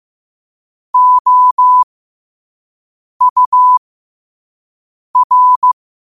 Morse code, heard as a keyed tone, about 15 wpm, spaced as for 5 wpm, 1000 Hz. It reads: OUR
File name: qotdfiles / 5wpm